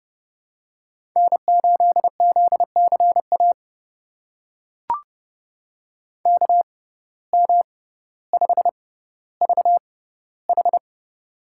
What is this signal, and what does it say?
Morse code, heard as a keyed tone, about 30 wpm, spaced as for 15 wpm, 700 Hz. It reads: N8ZCA KM5VH
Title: USA Call Signs - 5 Characters - 6x Character Spacing - 30wpm